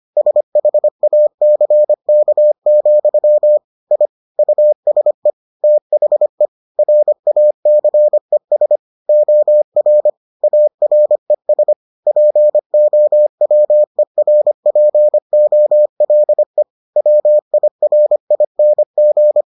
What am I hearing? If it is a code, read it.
Morse code, heard as a keyed tone, about 25 wpm, 600 Hz. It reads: SHACK, I USE THE RACES OR ARES POWERPOLE WIRING